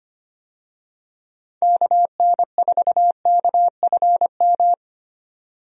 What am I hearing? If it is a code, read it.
Morse code, heard as a keyed tone, about 25 wpm, 700 Hz. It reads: KN4KFM
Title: USA Call Signs - 6 Characters - Repeated 3x - 25wpm